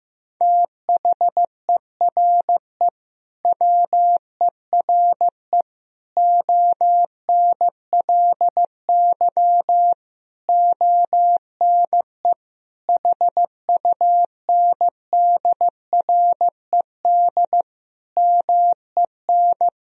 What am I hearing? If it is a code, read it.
Morse code, heard as a keyed tone, about 15 wpm, 700 Hz. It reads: THERE WERE ONLY ONE HUNDRED MEN
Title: Sentences from Top 1000 Words - Review Part 2 - 15wpm